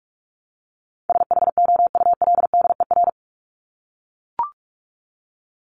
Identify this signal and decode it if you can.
Morse code, heard as a keyed tone, about 44 wpm, 700 Hz. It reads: SHOULDER
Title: Top 1501-1750 Words 45wpm